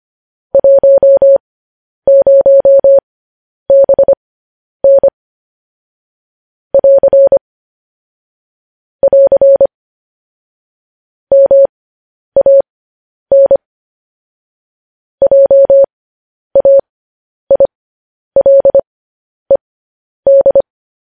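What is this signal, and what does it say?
Morse code, heard as a keyed tone, about 25 wpm, spaced as for 10 wpm, 550 Hz. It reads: 10BN <AR> <AR> MAN JAILED
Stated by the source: Morse code news headlines practice at 10 wpm.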